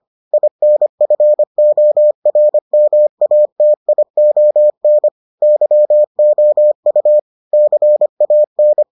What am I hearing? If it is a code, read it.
Morse code, heard as a keyed tone, about 25 wpm, 600 Hz. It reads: INFORMATION YOU CAN